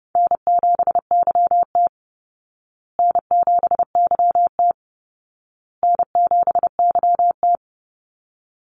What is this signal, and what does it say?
Morse code, heard as a keyed tone, about 30 wpm, 700 Hz. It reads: N7YT N7YT N7YT
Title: Call Signs Repeated 2x - 30wpm